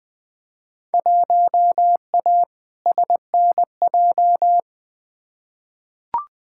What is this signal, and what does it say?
Morse code, heard as a keyed tone, about 20 wpm, 700 Hz. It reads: 1A SNJ
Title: Field Day Exchanges - 20wpm